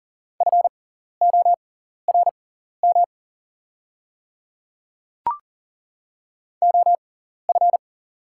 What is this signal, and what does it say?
Morse code, heard as a keyed tone, about 40 wpm, spaced as for 14 wpm, 700 Hz. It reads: FORM OF
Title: 6x Character Spacing - Top 300 Words - 40wpm